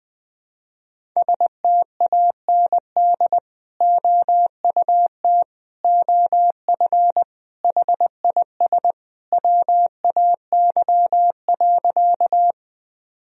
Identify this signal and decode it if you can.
Morse code, heard as a keyed tone, about 21 wpm, 700 Hz. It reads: STAND OUT OF HIS WAY.
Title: Sentences from Top 1000 Words - Review Part 1 - 20wpm